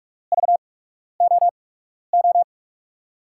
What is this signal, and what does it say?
Morse code, heard as a keyed tone, about 46 wpm, spaced as for 12 wpm, 700 Hz. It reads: VOO
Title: USA Call Signs - 6 Characters - 8x Character Spacing - 45wpm